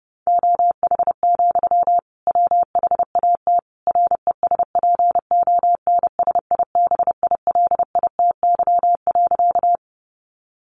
Morse code, heard as a keyed tone, about 30 wpm, 700 Hz. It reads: OH, WHAT RESPONSIBILITY.
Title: Sentences from Top 1200 Words 30wpm